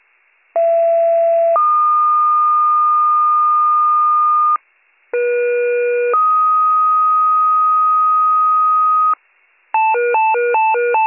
Two-tone page, heard as a one-second tone, then a three-second tone, about 670 then 1220 Hz, about 480 then 1250 Hz. It is heard from 0.6 to 9.1 s.